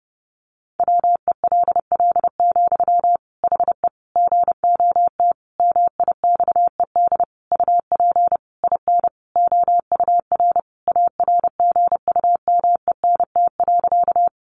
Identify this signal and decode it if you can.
Morse code, heard as a keyed tone, about 30 wpm, 700 Hz. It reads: WELL, HE GOT MIXED UP IN OUR ARGUMENT.
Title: Sentences from Top 1500 Words 30wpm